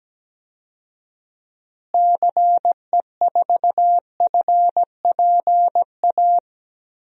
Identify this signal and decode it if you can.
Morse code, heard as a keyed tone, about 17 wpm, 700 Hz. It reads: CE4FPA
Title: DXCC-100 Call Signs - 6 Characters - Repeated 2x - 17wpm